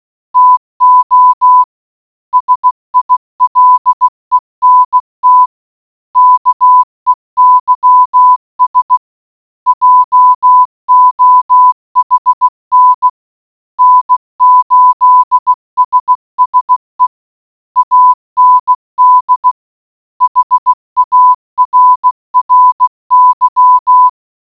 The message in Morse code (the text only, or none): TO SILENT KEYS JOHN N8SSE AND HARRY